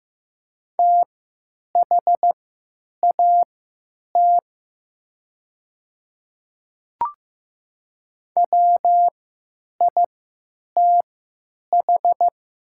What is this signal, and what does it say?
Morse code, heard as a keyed tone, about 15 wpm, spaced as for 9 wpm, 700 Hz. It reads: THAT WITH
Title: 3x Character Spacing - Top 100 Words - 15wpm